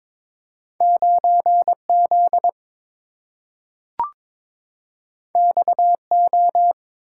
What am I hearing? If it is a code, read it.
Morse code, heard as a keyed tone, about 22 wpm, 690 Hz. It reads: 9Z XO
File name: practice / 22wpm / Two Letter-Number Combinations - 22wpm